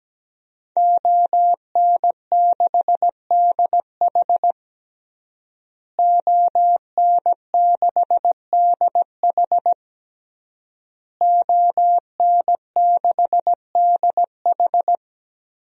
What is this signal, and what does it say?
Morse code, heard as a keyed tone, about 17 wpm, 700 Hz. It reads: ON6DH ON6DH ON6DH
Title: DXCC-100 Call Signs - 5 Characters - Repeated 3x - 17wpm